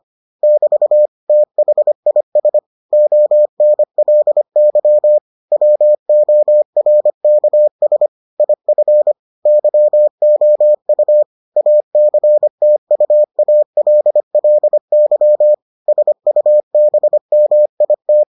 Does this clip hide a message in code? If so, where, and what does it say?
Yes: Morse code from 0.4 to 18.2 s, heard as a keyed tone, about 25 wpm, 600 Hz. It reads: <BT> THIS ONLY WORKS IF YOU ACTUALLY SUBMIT